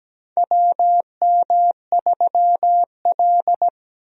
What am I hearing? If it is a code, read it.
Morse code, heard as a keyed tone, about 17 wpm, 700 Hz. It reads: WM3L
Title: USA Call Signs - 4 Characters - 17wpm